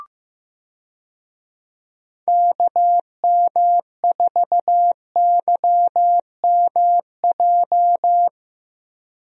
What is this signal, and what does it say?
Morse code, heard as a keyed tone, about 15 wpm, 700 Hz. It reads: KM4YMJ
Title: USA Call Signs - 6 Characters - Repeated 2x - 15wpm